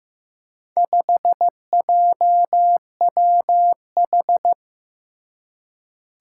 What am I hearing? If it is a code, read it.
Morse code, heard as a keyed tone, about 15 wpm, 700 Hz. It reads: 5JWH
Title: Four Challenging Letter-Number Combinations 15wpm